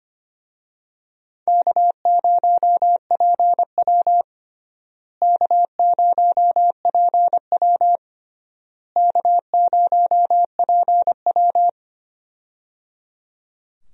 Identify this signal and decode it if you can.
Morse code, heard as a keyed tone, about 25 wpm, 700 Hz. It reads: K0PW K0PW K0PW
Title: USA Call Signs - 4 Characters - Repeated 3x - 25wpm